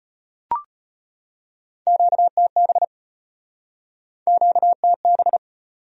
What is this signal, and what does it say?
Morse code, heard as a keyed tone, about 35 wpm, 700 Hz. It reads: QTB QTB